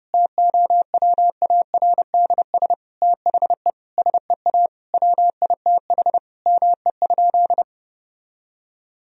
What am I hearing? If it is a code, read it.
Morse code, heard as a keyed tone, about 32 wpm, 700 Hz. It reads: TOWARDS THE SEA WITH ME?
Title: Sentences from Top 1000 Words - Review Part 2 - 30wpm